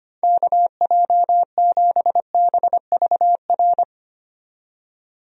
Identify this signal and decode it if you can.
Morse code, heard as a keyed tone, about 25 wpm, 700 Hz. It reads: KJ7BVR
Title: USA Call Signs - 6 Characters - 25wpm